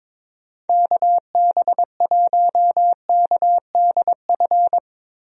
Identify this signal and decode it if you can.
Morse code, heard as a keyed tone, about 22 wpm, 700 Hz. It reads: KB1KDF
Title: USA Call Signs - 6 Characters - Repeated 3x - 22wpm